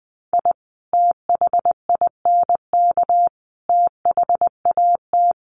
Morse code, heard as a keyed tone, about 20 wpm, 700 Hz. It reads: I THINK THAT
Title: Sets of 5 Words Encore2 20wpm